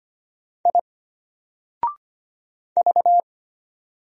Morse code, heard as a keyed tone, about 25 wpm, 700 Hz. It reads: I V
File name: Single-Letter-Number-25wpm